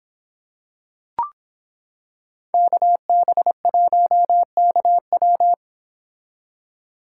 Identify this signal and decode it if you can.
Morse code, heard as a keyed tone, about 26 wpm, 700 Hz. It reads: KB1KW